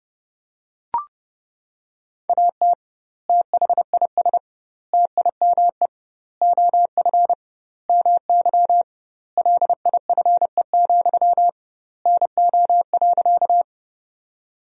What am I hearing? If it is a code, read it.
Morse code, heard as a keyed tone, about 32 wpm, 700 Hz. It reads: AT THIS TIME OF MY LIFE, NO.
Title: 2x Word-Spacing - Sentences from Top 300 Words - 30wpm